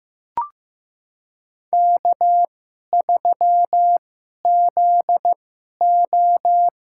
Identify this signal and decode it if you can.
Morse code, heard as a keyed tone, about 15 wpm, spaced as for 11 wpm, 700 Hz. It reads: K3ZO
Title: USA Call Signs - 4 Characters - 2x Character Spacing - 15wpm